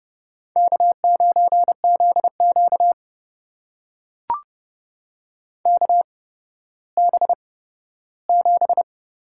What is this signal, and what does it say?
Morse code, heard as a keyed tone, about 31 wpm, spaced as for 15 wpm, 700 Hz. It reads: K9ZQ KB7
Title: USA Call Signs - 8x Character Spacing - 30wpm